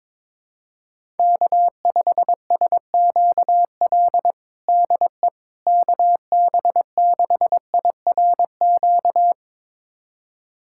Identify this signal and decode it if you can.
Morse code, heard as a keyed tone, about 22 wpm, 700 Hz. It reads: K5SQL DE KB6IRQ
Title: USA Callsign DE Callsign 22wpm